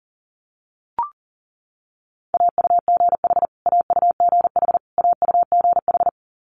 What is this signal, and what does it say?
Morse code, heard as a keyed tone, about 40 wpm, 700 Hz. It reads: AUGH AUGH AUGH
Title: Multi-letter Phonemes Repeated 3x 40wpm